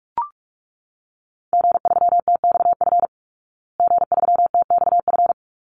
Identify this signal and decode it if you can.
Morse code, heard as a keyed tone, about 44 wpm, 700 Hz. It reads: G3TXF G3TXF
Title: Call Signs Repeated 3x - 45wpm